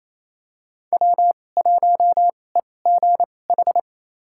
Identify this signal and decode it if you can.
Morse code, heard as a keyed tone, about 28 wpm, spaced as for 20 wpm, 700 Hz. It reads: W1EGH
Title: USA Call Signs - 2x Character Spacing - 28wpm